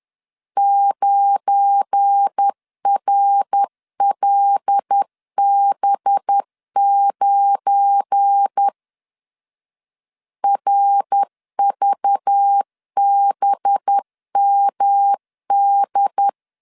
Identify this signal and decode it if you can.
Morse code, heard as a keyed tone, about 10 wpm, 800 Hz. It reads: 9RLB9 RVBMD
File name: LEZIONI / 7 GF9Z / 7 GF9 10wpm 2747